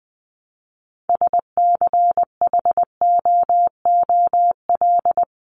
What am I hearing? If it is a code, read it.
Morse code, heard as a keyed tone, about 20 wpm, 700 Hz. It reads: SCHOOL
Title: Top 300 Words 20wpm